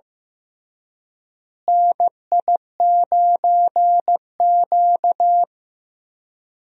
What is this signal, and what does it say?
Morse code, heard as a keyed tone, about 15 wpm, 700 Hz. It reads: NI9Q
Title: USA Call Signs - 4 Characters - Repeated 2x - 15wpm